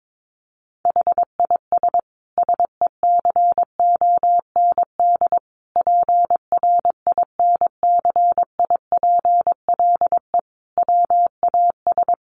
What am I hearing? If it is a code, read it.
Morse code, heard as a keyed tone, about 22 wpm, 700 Hz. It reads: HIS SECOND PRINCIPLE WAS
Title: Sentences from Top 1400 Words 22wpm